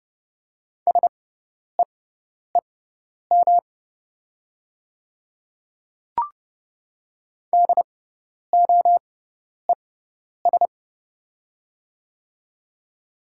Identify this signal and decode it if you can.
Morse code, heard as a keyed tone, about 30 wpm, spaced as for 10 wpm, 700 Hz. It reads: SEEM DOES
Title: 6x Character Spacing - Top 300 Words - 30wpm